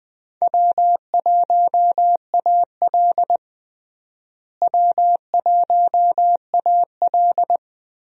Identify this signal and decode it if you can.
Morse code, heard as a keyed tone, about 20 wpm, 700 Hz. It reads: W1AL W1AL
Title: USA Call Signs - 4 Characters - Repeated 2x - 20wpm